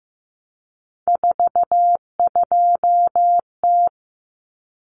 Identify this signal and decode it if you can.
Morse code, heard as a keyed tone, about 15 wpm, 700 Hz. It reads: 42T
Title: Common Cut Numbers - 3-Digit 15wpm